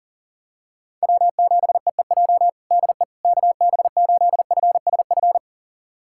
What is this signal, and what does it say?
Morse code, heard as a keyed tone, about 40 wpm, 700 Hz. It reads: W7EEJ DE KB8FSF
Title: USA Callsign DE Callsign 40wpm